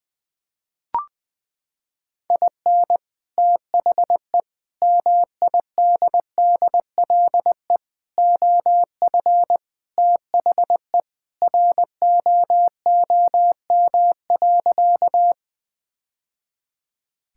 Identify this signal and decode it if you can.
Morse code, heard as a keyed tone, about 21 wpm, 700 Hz. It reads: IN THE MIDDLE OF THE ROOM.
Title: Sentences with 2-Word Phrases - 20wpm